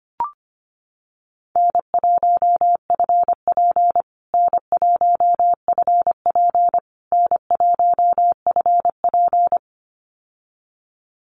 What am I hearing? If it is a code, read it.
Morse code, heard as a keyed tone, about 25 wpm, 700 Hz. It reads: N1FP N1FP N1FP
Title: USA Call Signs Repeated 3x 25wpm